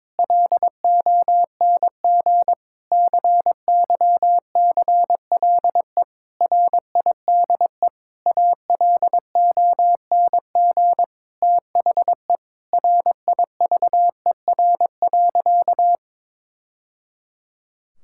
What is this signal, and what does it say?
Morse code, heard as a keyed tone, about 22 wpm, 700 Hz. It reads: LONG CYCLE RIDE ALONG THE RIVER.
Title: Sentences from Top 1000 Words - Review Part 6 - 22wpm